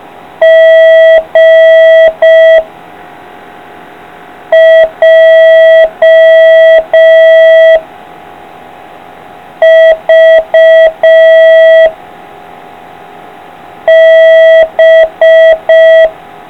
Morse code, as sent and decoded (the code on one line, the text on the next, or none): --. / .--- / ...- / -...
G J V B